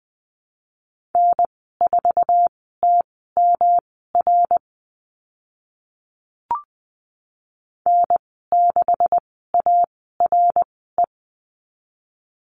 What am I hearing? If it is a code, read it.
Morse code, heard as a keyed tone, about 20 wpm, spaced as for 15 wpm, 700 Hz. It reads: N4TMR N6ARE
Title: 140 Lesson 23g - Letter f - Call Signs 20wpm